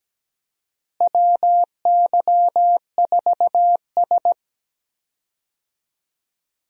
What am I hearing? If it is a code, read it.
Morse code, heard as a keyed tone, about 17 wpm, 700 Hz. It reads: WY4S
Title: USA Call Signs - 4 Characters - Repeated 2x - 17wpm